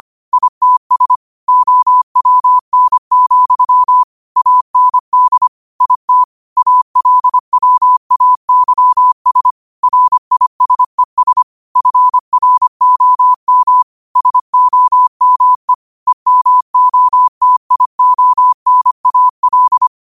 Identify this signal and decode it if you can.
Morse code, heard as a keyed tone, about 25 wpm, 1000 Hz. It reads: ITS OWN, AND IT ALWAYS RISES FROM SOME EMOTIONAL
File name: qotdfiles / 25wpm